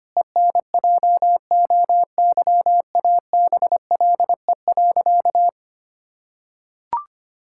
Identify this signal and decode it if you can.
Morse code, heard as a keyed tone, about 25 wpm, 700 Hz. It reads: ENJOYABLE.
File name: practice / 25wpm / Difficult Sentences with Hints - 25wpm